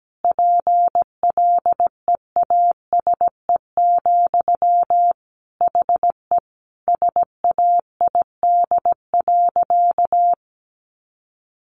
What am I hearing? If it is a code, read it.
Morse code, heard as a keyed tone, about 17 wpm, 700 Hz. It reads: PLEASE, HE SAID.
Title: Sentences from Top 1500 Words 17wpm